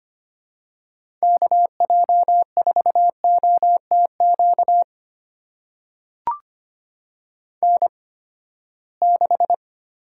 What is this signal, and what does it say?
Morse code, heard as a keyed tone, about 25 wpm, spaced as for 17 wpm, 700 Hz. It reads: KJ4OTQ N6
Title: USA Call Signs - 8x Character Spacing - 25wpm